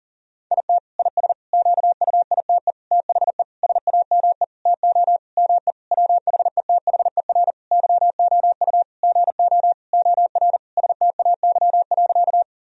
Morse code, heard as a keyed tone, about 40 wpm, 700 Hz. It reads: IT IS QUITE THE SAME TO ME WHETHER YOU GO OR STAY.
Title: Sentences from Top 1000 Words - Review Part 3 - 40wpm